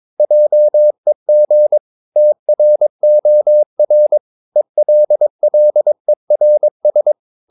Morse code, heard as a keyed tone, about 22 wpm, 600 Hz. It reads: JEG TROR ELLERS
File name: Ferie QRS 22wpm